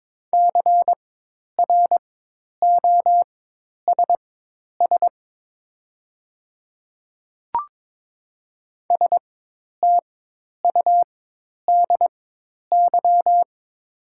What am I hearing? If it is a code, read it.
Morse code, heard as a keyed tone, about 22 wpm, spaced as for 10 wpm, 700 Hz. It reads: CROSS STUDY
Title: 4x Character Spacing - Top 300 Words - 22wpm